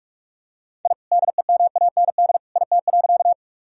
Morse code, heard as a keyed tone, about 46 wpm, 700 Hz. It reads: I DEMAND IT.
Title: Sentences from Top 1000 Words - Review Part 4 - 45wpm